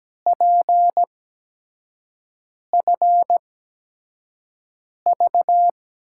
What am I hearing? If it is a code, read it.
Morse code, heard as a keyed tone, about 17 wpm, spaced as for 5 wpm, 700 Hz. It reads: PFV
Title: USA Call Signs - 5 Characters - 8x Character Spacing - 17wpm